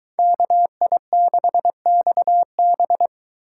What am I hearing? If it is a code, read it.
Morse code, heard as a keyed tone, about 23 wpm, 700 Hz. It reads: KI6XB